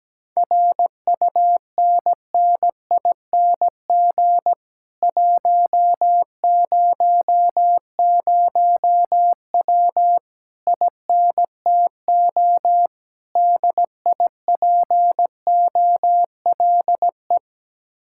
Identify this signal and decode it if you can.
Morse code, heard as a keyed tone, about 17 wpm, 700 Hz. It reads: RUNNING 100W INTO DIPOLE
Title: Speed-Racing Common QSO Phrases Encore - 17wpm